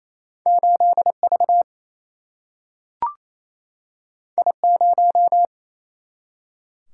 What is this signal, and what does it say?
Morse code, heard as a keyed tone, about 28 wpm, 700 Hz. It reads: 8V I0
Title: Single-Letter Single-Number 28wpm